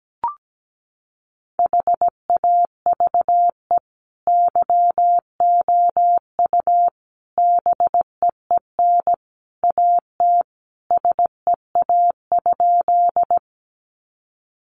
Morse code, heard as a keyed tone, about 17 wpm, 700 Hz. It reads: HAVE YOU BEEN AT SEA?
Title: Sentences from Top 300 Words 17wpm